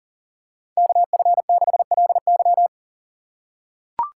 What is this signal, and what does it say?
Morse code, heard as a keyed tone, about 40 wpm, 700 Hz. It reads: KF6LY
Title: USA Call Signs - 5 Characters - 4x Character Spacing - 40wpm